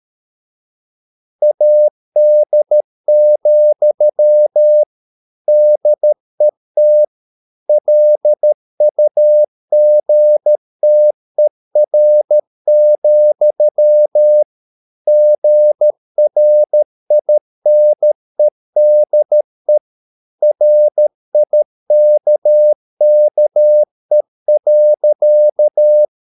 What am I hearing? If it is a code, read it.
Morse code, heard as a keyed tone, about 13 wpm, 600 Hz. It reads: AD, DET LUGTER, GRINEDE RIKKE.
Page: Ferie 13 wpm